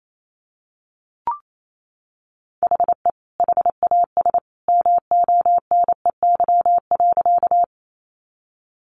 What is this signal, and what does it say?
Morse code, heard as a keyed tone, about 28 wpm, 700 Hz. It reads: HE HAS MONEY.
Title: Sentences from Top 400 Words 28wpm